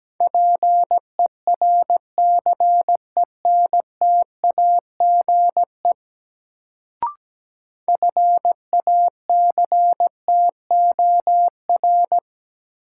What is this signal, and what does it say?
Morse code, heard as a keyed tone, about 17 wpm, 700 Hz. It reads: PERCENTAGE FACTOR
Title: Sentences from Top 1300 Words - New Word Drill Alt - 17wpm